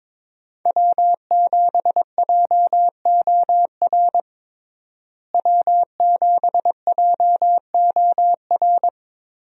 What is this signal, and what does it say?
Morse code, heard as a keyed tone, about 22 wpm, 700 Hz. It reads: W7JOR W7JOR
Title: USA Call Signs - 5 Characters - Repeated 2x - 22wpm